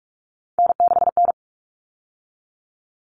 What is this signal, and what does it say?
Morse code, heard as a keyed tone, about 44 wpm, 700 Hz. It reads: N6N
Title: All Cut Numbers - 3-Digit 45wpm